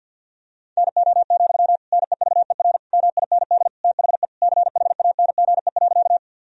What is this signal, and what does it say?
Morse code, heard as a keyed tone, about 50 wpm, 700 Hz. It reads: NO, NEVER MIND THE CHANGE.
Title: Sentences from Top 1000 Words - Review Part 1 - 50wpm